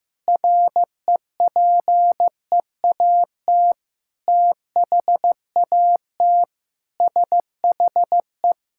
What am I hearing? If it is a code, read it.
Morse code, heard as a keyed tone, about 15 wpm, 700 Hz. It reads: REPEAT THAT SHE
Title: Sentences from Top 1600 Words 15wpm